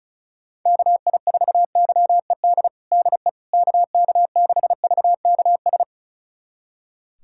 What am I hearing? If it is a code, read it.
Morse code, heard as a keyed tone, about 36 wpm, 700 Hz. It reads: KI4YED DE KK6VKS
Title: USA Callsign DE Callsign - 6 Characters - 35wpm